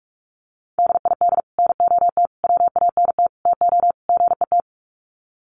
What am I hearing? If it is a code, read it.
Morse code, heard as a keyed tone, about 44 wpm, 700 Hz. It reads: DID NOT WANT TO GET
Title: Sets of 5 Words Encore2 45wpm